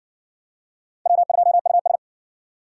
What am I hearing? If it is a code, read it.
Morse code, heard as a keyed tone, about 60 wpm, 700 Hz. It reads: W2US